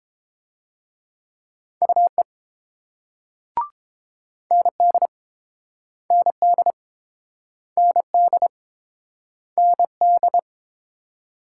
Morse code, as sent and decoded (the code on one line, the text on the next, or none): ..- . / -. -.. / -. -.. / -. -.. / -. -..
UE ND ND ND ND